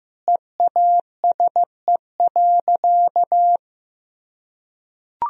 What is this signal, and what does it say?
Morse code, heard as a keyed tone, about 15 wpm, 700 Hz. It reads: EASE.
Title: Sentences from Top 1000 Words - Review Part 1 - 15wpm